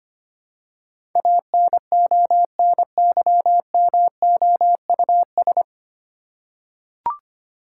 Morse code, heard as a keyed tone, about 25 wpm, 700 Hz. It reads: ANONYMOUS
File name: Nine Letter Words - 25wpm